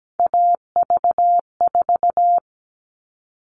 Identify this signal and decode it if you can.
Morse code, heard as a keyed tone, about 17 wpm, 700 Hz. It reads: AV4
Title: All Cut Numbers - 3-Digit 17wpm